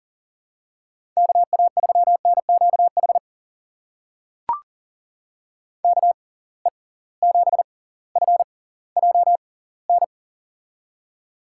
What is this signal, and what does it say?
Morse code, heard as a keyed tone, about 37 wpm, spaced as for 20 wpm, 700 Hz. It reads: KA3NQH KE7FJN
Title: USA Call Signs - 6 Characters - 6x Character Spacing - 40wpm